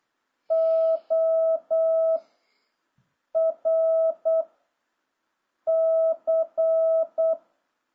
Morse code, heard as a keyed tone, about 8 wpm, 650 Hz. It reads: O R C